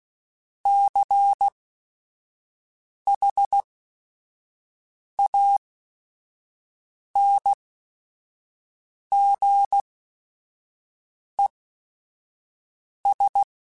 Morse code, heard as a keyed tone, about 16 wpm, spaced as for 5 wpm, 800 Hz. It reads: CHANGES
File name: Technical Subjects / Technical Subjects-mp3s / Reactance @5